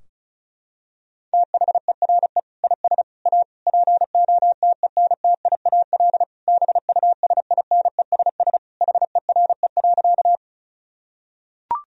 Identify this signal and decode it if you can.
Morse code, heard as a keyed tone, about 36 wpm, 700 Hz. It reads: THERE IS A POTENTIAL BUSINESS HERE.
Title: Sentences from Top 1000 Words - Review Part 4 - 35wpm